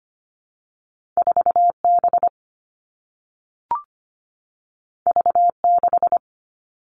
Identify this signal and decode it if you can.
Morse code, heard as a keyed tone, about 25 wpm, 700 Hz. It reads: 4 B V 6
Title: Two Challenging Letter-Number Combinations 25wpm